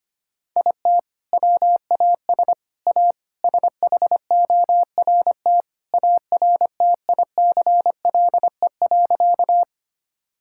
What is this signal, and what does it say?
Morse code, heard as a keyed tone, about 25 wpm, 700 Hz. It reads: IT WAS A SHORT ARTICLE.
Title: Sentences from Top 1000 Words - Review Part 5 - 25wpm